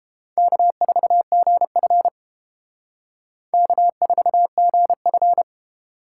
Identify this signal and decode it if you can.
Morse code, heard as a keyed tone, about 30 wpm, 700 Hz. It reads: K4GF K4GF